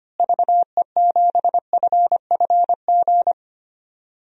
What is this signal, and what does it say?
Morse code, heard as a keyed tone, about 25 wpm, 700 Hz. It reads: VE7FFG
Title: DXCC-100 Call Signs - 6 Characters - Repeated 2x - 25wpm